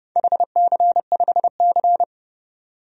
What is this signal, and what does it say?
Morse code, heard as a keyed tone, about 30 wpm, 700 Hz. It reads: HC5C
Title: DXCC-100 Call Signs - 4 Characters - 30wpm